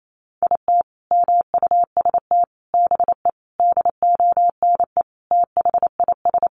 Morse code, heard as a keyed tone, about 28 wpm, 700 Hz. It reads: IT MUST BE DONE THIS
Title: Sentences from Top 1500 Words 28wpm